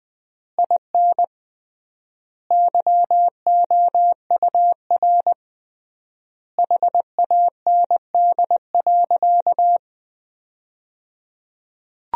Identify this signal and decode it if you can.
Morse code, heard as a keyed tone, about 20 wpm, 700 Hz. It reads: IN YOUR HAND.
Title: 3x Word-Spacing - Sentences from Top 400 Words - 20wpm